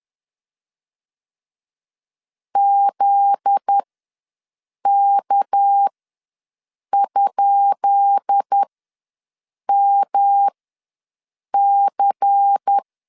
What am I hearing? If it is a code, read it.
Morse code, heard as a keyed tone, about 10 wpm, spaced as for 6 wpm, 800 Hz. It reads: ZK?MC
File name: LEZIONI / 10 SIGNS / 10B INT ws 10wpm 4134